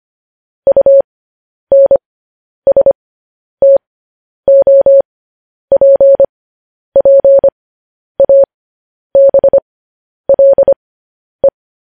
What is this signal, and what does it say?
Morse code, heard as a keyed tone, about 25 wpm, spaced as for 10 wpm, 550 Hz. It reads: UNSTOPPABLE
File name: podcast / News headlines-1x-25x10